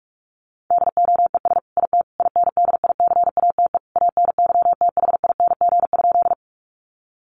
Morse code, heard as a keyed tone, about 46 wpm, 700 Hz. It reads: DOES IT INDICATE ANYTHING?
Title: Sentences from Top 1300 Words 45wpm